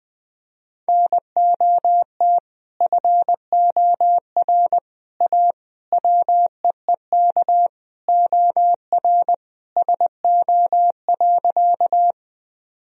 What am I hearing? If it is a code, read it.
Morse code, heard as a keyed tone, about 20 wpm, 700 Hz. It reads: NOT FOR A WEEK OR SO.
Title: Sentences from Top 1000 Words - Review Part 2 - 20wpm